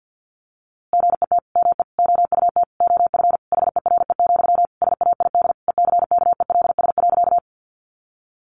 Morse code, heard as a keyed tone, about 50 wpm, 700 Hz. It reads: GET ME OUT OF HERE, SAID ECKELS.
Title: A Sound Of Thunder 50wpm